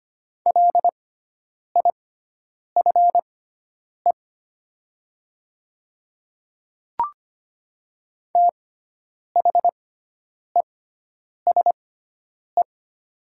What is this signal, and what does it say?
Morse code, heard as a keyed tone, about 25 wpm, spaced as for 9 wpm, 700 Hz. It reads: LIFE THESE
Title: 6x Character Spacing - Top 500 Words - 25wpm